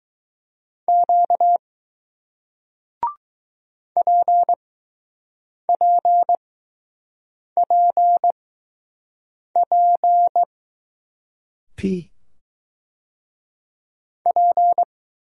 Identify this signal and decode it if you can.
Morse code, heard as a keyed tone, about 20 wpm, spaced as for 6 wpm, 700 Hz. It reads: Q PPPP P